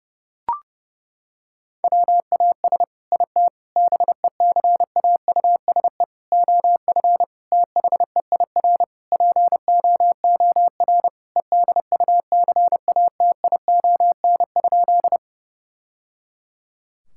Morse code, heard as a keyed tone, about 32 wpm, 700 Hz. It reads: WAS IT BECAUSE OF THEIR POOR EDUCATION?
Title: Sentences from Top 1000 Words - Review Part 6 - 30wpm